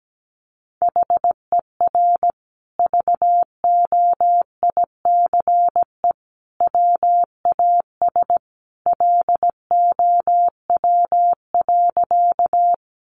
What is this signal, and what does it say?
Morse code, heard as a keyed tone, about 17 wpm, 700 Hz. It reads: HER VOICE WAS LOW.
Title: Sentences from Top 500 Words 17wpm